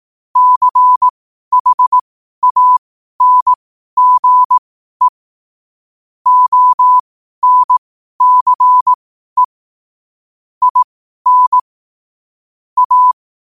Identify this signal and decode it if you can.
Morse code, heard as a keyed tone, about 18 wpm, spaced as for 12 wpm, 1000 Hz. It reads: CHANGE ONCE IN A
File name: qotdfiles / 12wpm